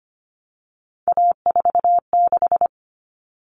Morse code, heard as a keyed tone, about 25 wpm, 700 Hz. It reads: A46
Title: Common Cut Numbers - 3-Digit 25wpm